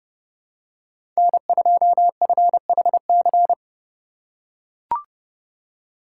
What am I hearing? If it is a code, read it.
Morse code, heard as a keyed tone, about 30 wpm, 700 Hz. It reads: N2FHC
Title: USA Call Signs - 4x Character Spacing - 30wpm